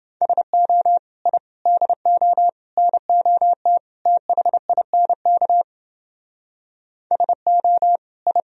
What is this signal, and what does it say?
Morse code, heard as a keyed tone, about 28 wpm, 700 Hz. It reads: SO I DO NOT THINK SO I